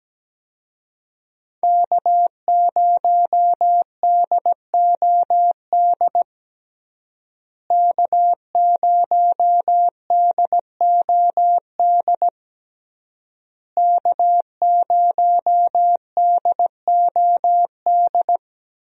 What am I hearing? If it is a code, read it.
Morse code, heard as a keyed tone, about 17 wpm, 700 Hz. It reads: K0DOD K0DOD K0DOD
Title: USA Call Signs - 5 Characters - Repeated 3x - 17wpm